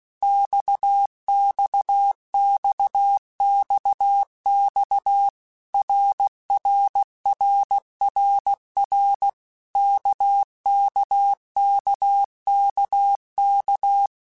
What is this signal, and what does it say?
Morse code, heard as a keyed tone, about 16 wpm, 800 Hz. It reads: XXXXX RRRRR KKKKK
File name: Basic Letter Groups By7Groups / RKPX / RKPX@15